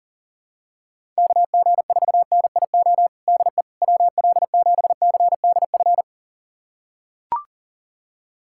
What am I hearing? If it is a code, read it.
Morse code, heard as a keyed tone, about 40 wpm, 700 Hz. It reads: KG4NIO DE WL7CDF
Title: USA Callsign DE Callsign - 6 Characters - 40wpm